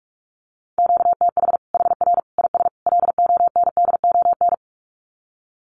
Morse code, heard as a keyed tone, about 44 wpm, 700 Hz. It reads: QTH HR IS LONDON
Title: Common QSO Phrases 45wpm